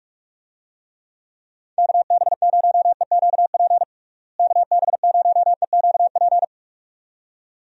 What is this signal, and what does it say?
Morse code, heard as a keyed tone, about 45 wpm, 700 Hz. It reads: KB0EQP KB0EQP
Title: USA Call Signs - 6 Characters - Repeated 2x - 45wpm